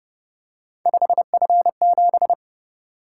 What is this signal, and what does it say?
Morse code, heard as a keyed tone, about 31 wpm, 700 Hz. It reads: 5F7